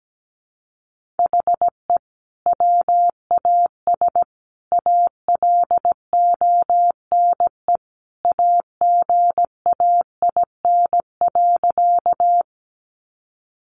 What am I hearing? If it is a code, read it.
Morse code, heard as a keyed tone, about 17 wpm, 700 Hz. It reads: HE WAS ALONE AGAIN.
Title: Sentences from Top 1000 Words 17wpm